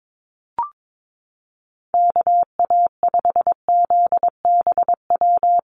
Morse code, heard as a keyed tone, about 22 wpm, 700 Hz. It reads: KA5ZBW
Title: USA Call Signs Repeated 2x 22wpm